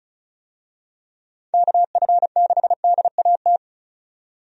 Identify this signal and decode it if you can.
Morse code, heard as a keyed tone, about 34 wpm, 700 Hz. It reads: KF6DAT
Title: USA Call Signs - 6 Characters - 35wpm